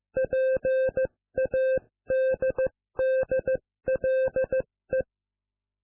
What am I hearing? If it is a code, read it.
Morse code, heard as a keyed tone, about 15 wpm, 520 Hz. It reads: PADDLE